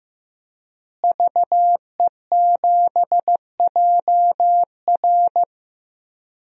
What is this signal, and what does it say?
Morse code, heard as a keyed tone, about 16 wpm, 700 Hz. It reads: VE7JR
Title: DXCC-100 Call Signs - 5 Characters - Repeated 2x - 15wpm